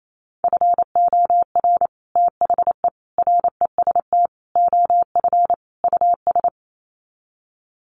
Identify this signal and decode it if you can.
Morse code, heard as a keyed tone, about 28 wpm, 700 Hz. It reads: FOR THE REST OF US
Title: Sets of 5 Words Encore 28wpm